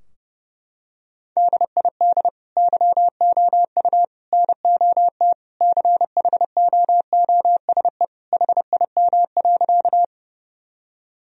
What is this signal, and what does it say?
Morse code, heard as a keyed tone, about 32 wpm, 700 Hz. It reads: DID YOU NOT CHOOSE HIM.
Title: Sentences from Top 1000 Words - Review Part 4 - 30wpm